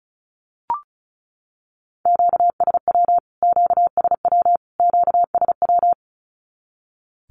Morse code, heard as a keyed tone, about 34 wpm, 700 Hz. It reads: QSW QSW QSW
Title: Q-Codes Repeated 3x 35wpm